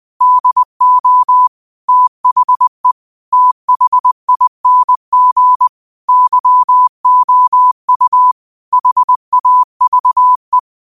Morse code, heard as a keyed tone, about 20 wpm, 1000 Hz. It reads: DO THE THING YOU HAVE
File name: qotdfiles / 20wpm